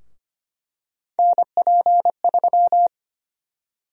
Morse code, heard as a keyed tone, about 25 wpm, 700 Hz. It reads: NP3
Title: USA Call Sign Prefixes - 25wpm